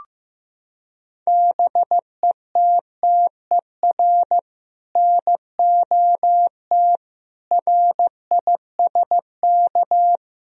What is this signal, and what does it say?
Morse code, heard as a keyed tone, about 15 wpm, 700 Hz. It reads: BETTER NOT RISK
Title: Sentences from Top 1000 Words - Review Part 4 - 15wpm